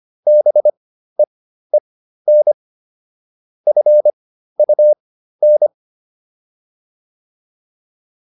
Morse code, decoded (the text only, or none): BEEN FUN